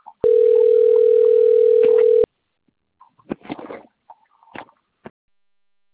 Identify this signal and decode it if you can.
Ringback tone (North America), 440 + 480 Hz.
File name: rg-1600-unknown- - - .70941